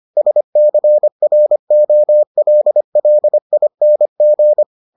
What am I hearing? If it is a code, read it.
Morse code, heard as a keyed tone, about 25 wpm, 600 Hz. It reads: SCROLLING